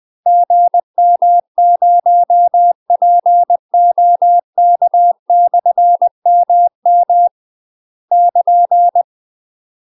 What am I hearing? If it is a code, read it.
Morse code, heard as a keyed tone, about 20 wpm, 700 Hz. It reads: GM0POK/MM <KN>
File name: UK MASTER QSOs TRAINING / 20WPM / UK MASTER QSOs 0050